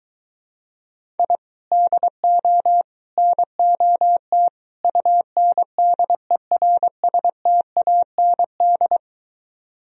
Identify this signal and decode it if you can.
Morse code, heard as a keyed tone, about 23 wpm, 700 Hz. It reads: I DO NOT UNDERSTAND